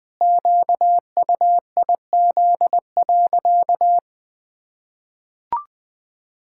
Morse code, heard as a keyed tone, about 20 wpm, 700 Hz. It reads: QUIZ.
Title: Pangram Challenge v2 - 20wpm